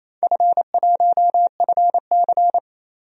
Morse code, heard as a keyed tone, about 28 wpm, 700 Hz. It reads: F1FC
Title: DXCC-100 Call Signs - 4 Characters - 28wpm